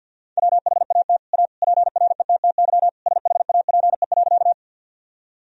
Morse code, heard as a keyed tone, about 50 wpm, 700 Hz. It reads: WHAT A PRETTY SHAPE.
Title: Sentences from Top 1000 Words - Review Part 3 - 50wpm